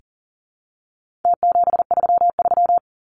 Morse code, heard as a keyed tone, about 40 wpm, 700 Hz. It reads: T733
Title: Numbers - Contest Style - 4-Digit 40wpm